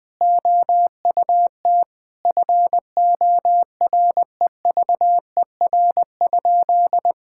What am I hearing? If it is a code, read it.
Morse code, heard as a keyed tone, about 20 wpm, 700 Hz. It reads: OUT FOREVER?
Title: Sentences from Top 1600 Words 20wpm